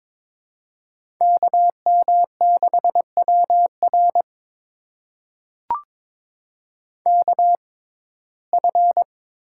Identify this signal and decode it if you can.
Morse code, heard as a keyed tone, about 23 wpm, spaced as for 16 wpm, 700 Hz. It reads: KM6WR KF
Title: USA Call Signs - 5 Characters - 6x Character Spacing - 22wpm